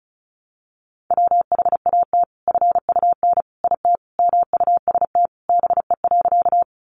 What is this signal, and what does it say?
Morse code, heard as a keyed tone, about 34 wpm, 700 Hz. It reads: WHAT FUN IT MUST BE.
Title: Sentences from Top 700 Words 35wpm